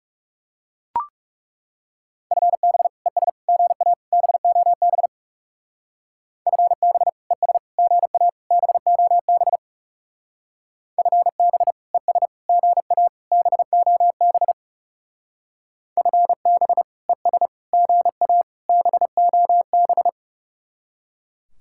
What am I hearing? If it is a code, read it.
Morse code, heard as a keyed tone, about 38 wpm, 700 Hz. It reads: FB ES GA BOB FB ES GA BOB FB ES GA BOB FB ES GA BOB